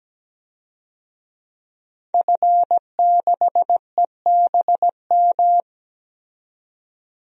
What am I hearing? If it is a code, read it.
Morse code, heard as a keyed tone, about 17 wpm, 700 Hz. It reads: F6EBM
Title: DXCC-100 Call Signs - 5 Characters - 17wpm